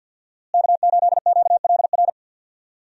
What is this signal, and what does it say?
Morse code, heard as a keyed tone, about 47 wpm, 700 Hz. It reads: K8QLR
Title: USA Call Signs - 5 Characters - Repeated 3x - 50wpm